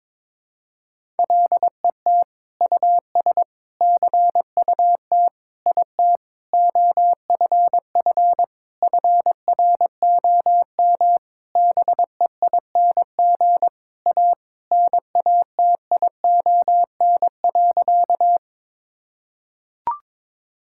Morse code, heard as a keyed tone, about 21 wpm, 700 Hz. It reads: LET US CUT IT OFF FROM BEING A NATION.
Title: Sentences from Top 1000 Words - Review Part 6 - 22wpm